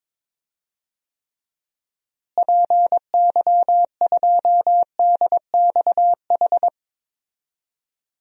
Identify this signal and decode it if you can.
Morse code, heard as a keyed tone, about 22 wpm, 700 Hz. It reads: PY2DXH
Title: DXCC-100 Call Signs - 6 Characters - Repeated 3x - 22wpm